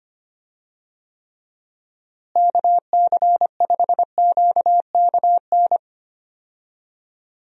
Morse code, heard as a keyed tone, about 25 wpm, 700 Hz. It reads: KC5QKN
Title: USA Call Signs - 6 Characters - Repeated 3x - 25wpm